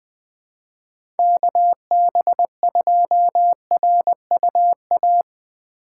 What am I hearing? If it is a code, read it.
Morse code, heard as a keyed tone, about 20 wpm, 700 Hz. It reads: KB2RUA
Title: USA Call Signs - 2x Character Spacing - 20wpm